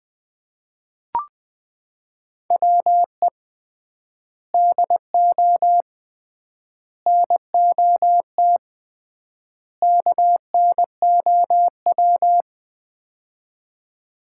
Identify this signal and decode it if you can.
Morse code, heard as a keyed tone, about 21 wpm, 700 Hz. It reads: WE DO NOT KNOW
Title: 3x Word-Spacing - Sets of 4 Words - 20wpm